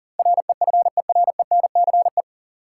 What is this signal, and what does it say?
Morse code, heard as a keyed tone, about 40 wpm, 700 Hz. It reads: REFERENCE
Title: Sentences from Top 1200 Words - New Word Drill Alt - 40wpm